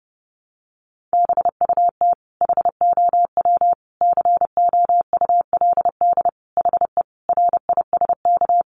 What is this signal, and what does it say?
Morse code, heard as a keyed tone, about 30 wpm, 700 Hz. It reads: BUT HOW COULD HE RISK
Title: Sentences from Top 600 Words 30wpm